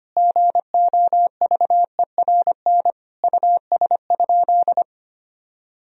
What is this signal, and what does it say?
Morse code, heard as a keyed tone, about 25 wpm, 700 Hz. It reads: GOVERN US?
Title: Sentences from Top 1000 Words - Review Part 2 - 25wpm